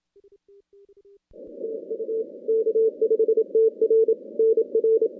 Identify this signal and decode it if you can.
Morse code, heard as a keyed tone, about 27 wpm, 440 Hz. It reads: TU K5TR NR
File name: contests / CW-SS-k5tr- -2100